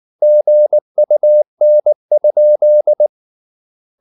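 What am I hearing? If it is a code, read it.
Morse code, heard as a keyed tone, about 19 wpm, 600 Hz. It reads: GUN?
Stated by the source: Morse code at 19 wpm from CW med Gnister, reading Ken Follett's novel Never in English.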